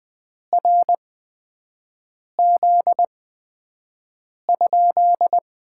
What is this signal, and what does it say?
Morse code, heard as a keyed tone, about 20 wpm, 700 Hz. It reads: R Z ?
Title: Top 90 QSO Elements - 8x Character Spacing - 20wpm